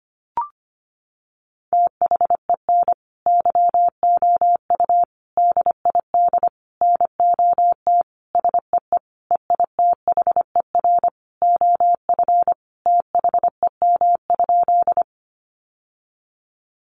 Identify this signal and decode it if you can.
Morse code, heard as a keyed tone, about 25 wpm, 700 Hz. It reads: THEN YOU DID NOT SEE EITHER OF THEM?
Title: Sentences from Top 600 Words 25wpm